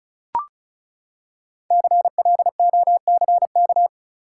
Morse code, heard as a keyed tone, about 34 wpm, 700 Hz. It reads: CLOCK
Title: Five Letter Words - 35wpm